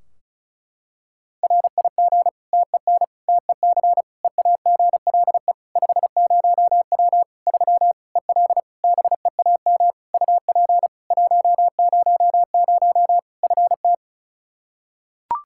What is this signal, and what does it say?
Morse code, heard as a keyed tone, about 36 wpm, 700 Hz. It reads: RIG TEN TEC EAGLE 50W 3 EL BEAM UP 100 FT
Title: Practice QSO - Exchange 2 - Info - 35wpm